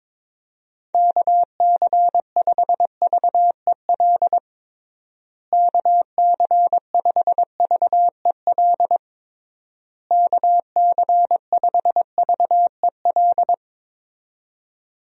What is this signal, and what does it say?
Morse code, heard as a keyed tone, about 22 wpm, 700 Hz. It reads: KC5VEL KC5VEL KC5VEL
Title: USA Call Signs - 6 Characters - Repeated 3x - 22wpm